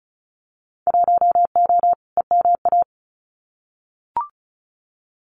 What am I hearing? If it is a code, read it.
Morse code, heard as a keyed tone, about 34 wpm, 700 Hz. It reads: 1O EMA
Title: Winter Field Day Exchanges 35wpm